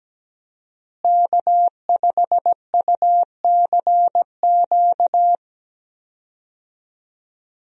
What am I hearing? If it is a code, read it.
Morse code, heard as a keyed tone, about 17 wpm, 700 Hz. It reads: K5UCQ
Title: USA Call Signs - 5 Characters - Repeated 3x - 17wpm